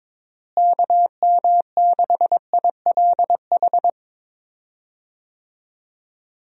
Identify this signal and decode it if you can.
Morse code, heard as a keyed tone, about 22 wpm, 700 Hz. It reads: KM6ILH
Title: USA Call Signs - 6 Characters - Repeated 2x - 22wpm